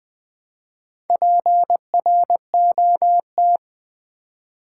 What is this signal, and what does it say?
Morse code, heard as a keyed tone, about 20 wpm, 700 Hz. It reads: PROT
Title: Build A Word - 2 Letters At A Time - 8-Letter Words - 20wpm